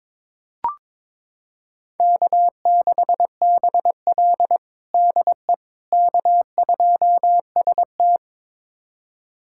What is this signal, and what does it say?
Morse code, heard as a keyed tone, about 22 wpm, 700 Hz. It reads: K6BL DE K2ST
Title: USA Callsign DE Callsign - 4 Characters - 22wpm